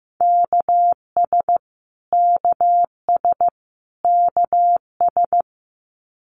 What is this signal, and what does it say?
Morse code, heard as a keyed tone, about 15 wpm, 700 Hz. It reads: KS KS KS
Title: US States Repeated 3x 15wpm